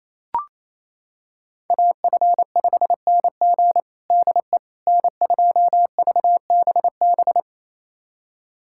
Morse code, heard as a keyed tone, about 28 wpm, 700 Hz. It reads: AF5NG DE N2VBB
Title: USA Callsign DE Callsign - 5 Characters - 28wpm